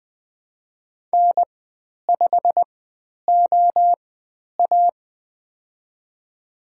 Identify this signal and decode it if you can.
Morse code, heard as a keyed tone, about 20 wpm, spaced as for 10 wpm, 700 Hz. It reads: N5OA
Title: 095 Lesson 16g - Letter c - Call Signs 20wpm